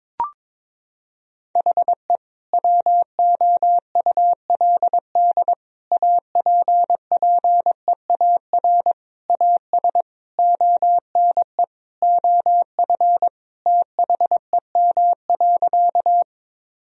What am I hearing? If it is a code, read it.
Morse code, heard as a keyed tone, about 22 wpm, 700 Hz. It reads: HE WOULD APPEAR AS ONE OF THEM.
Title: Sentences from Top 1000 Words - Review Part 2 - 22wpm